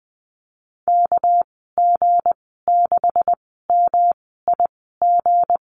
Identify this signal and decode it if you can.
Morse code, heard as a keyed tone, about 20 wpm, spaced as for 15 wpm, 700 Hz. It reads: KG6MIG
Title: 200 Lesson 32g - Letter k - Call Signs 20wpm